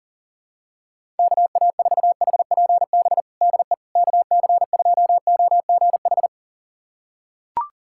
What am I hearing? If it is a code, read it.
Morse code, heard as a keyed tone, about 37 wpm, 700 Hz. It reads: KA4HPB DE KC2OGH
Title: USA Callsign DE Callsign 40wpm